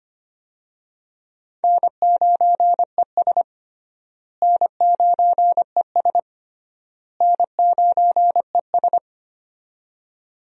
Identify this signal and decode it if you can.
Morse code, heard as a keyed tone, about 25 wpm, 700 Hz. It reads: N9ES N9ES N9ES
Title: USA Call Signs - 4 Characters - Repeated 3x - 25wpm